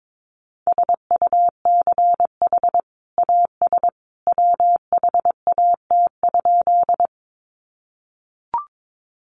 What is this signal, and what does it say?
Morse code, heard as a keyed tone, about 22 wpm, 700 Hz. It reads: SUCH AS WHAT?
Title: Sentences from Top 200 Words 22wpm